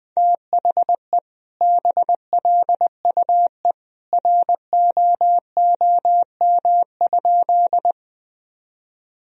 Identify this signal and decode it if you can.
Morse code, heard as a keyed tone, about 20 wpm, 700 Hz. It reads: THE BLUE ROOM?
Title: Sentences from Top 1000 Words - Review Part 3 - 20wpm